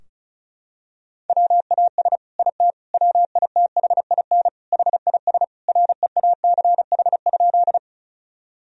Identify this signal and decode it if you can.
Morse code, heard as a keyed tone, about 36 wpm, 700 Hz. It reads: WAS IT WITHIN HIS REACH?
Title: Sentences from Top 1000 Words - Review Part 4 - 35wpm